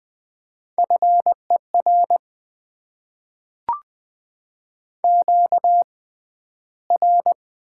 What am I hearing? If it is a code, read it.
Morse code, heard as a keyed tone, about 20 wpm, spaced as for 12 wpm, 700 Hz. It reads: FER QR
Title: Top 20 QSO Elements - 6x Character Spacing - 20wpm